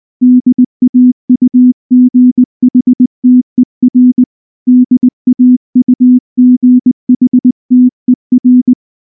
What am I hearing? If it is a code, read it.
Morse code, heard as a keyed tone, about 19 wpm, 250 Hz. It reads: DAUGHTER DAUGHTER
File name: morse / morse-training-20WPM-7